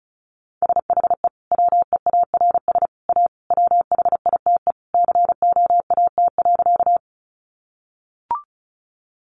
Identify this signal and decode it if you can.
Morse code, heard as a keyed tone, about 34 wpm, 700 Hz. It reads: SHE WEARS A WHITE COAT.
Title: Additional Easy Phrases 35wpm